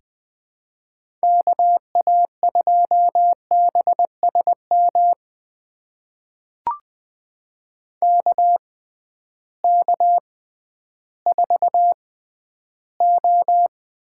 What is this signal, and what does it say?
Morse code, heard as a keyed tone, about 20 wpm, spaced as for 12 wpm, 700 Hz. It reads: KA2BSM KK4O
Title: USA Call Signs - 6 Characters - 6x Character Spacing - 20wpm